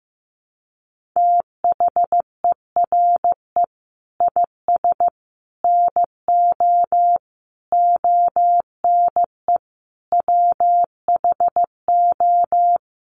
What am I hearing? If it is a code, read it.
Morse code, heard as a keyed tone, about 15 wpm, 700 Hz. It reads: THERE IS NO ONE WHO
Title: Sentences from Top 1000 Words 15wpm